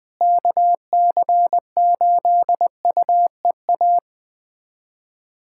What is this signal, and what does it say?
Morse code, heard as a keyed tone, about 20 wpm, 700 Hz. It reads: KC8UEA
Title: USA Callsign DE Callsign - 6 Characters - 20wpm